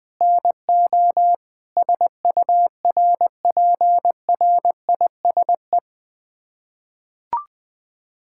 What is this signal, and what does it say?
Morse code, heard as a keyed tone, about 21 wpm, 700 Hz. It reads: NO SURPRISE
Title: Sets of 6 Words - 20wpm